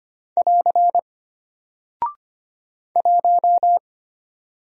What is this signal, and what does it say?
Morse code, heard as a keyed tone, about 25 wpm, 700 Hz. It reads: <AR> 1
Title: Single Character (Letter:Number:Punctuation:Prosign) Alt 25wpm